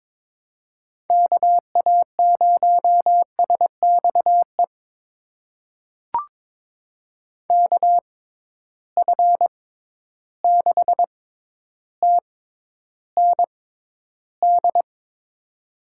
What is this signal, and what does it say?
Morse code, heard as a keyed tone, about 22 wpm, spaced as for 11 wpm, 700 Hz. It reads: KA0SXE KF6TND
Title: USA Call Signs - 6 Characters - 6x Character Spacing - 22wpm